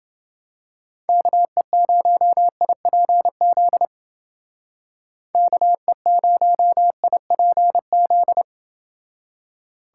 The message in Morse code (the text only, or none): KE0IPZ KE0IPZ